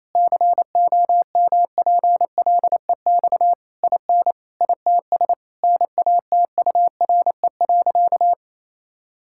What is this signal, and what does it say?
Morse code, heard as a keyed tone, about 28 wpm, 700 Hz. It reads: COMPLEX IN ITS NATURE.
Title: Sentences from Top 1000 Words - Review Part 6 - 28wpm